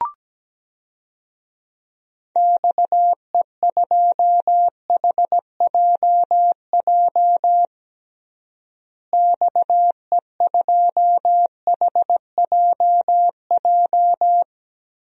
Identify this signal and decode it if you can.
Morse code, heard as a keyed tone, about 17 wpm, 700 Hz. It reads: XE2HJJ XE2HJJ
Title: DXCC-100 Call Signs - 6 Characters - Repeated 3x - 17wpm